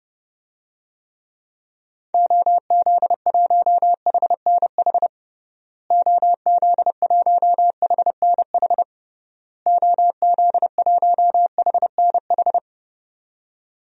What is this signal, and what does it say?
Morse code, heard as a keyed tone, about 31 wpm, 700 Hz. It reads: OZ1HNH OZ1HNH OZ1HNH
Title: DXCC-100 Call Signs - 6 Characters - Repeated 3x - 30wpm